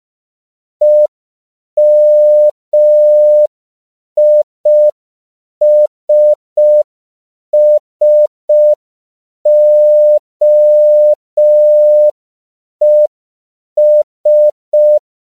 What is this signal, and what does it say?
Morse code, heard as a keyed tone, about 5 wpm, 600 Hz. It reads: EMISSOES
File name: CW / 5PPM